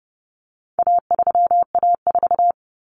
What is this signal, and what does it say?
Morse code, heard as a keyed tone, about 30 wpm, 700 Hz. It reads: A3A4
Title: Common Cut Numbers - 4-Digit 30wpm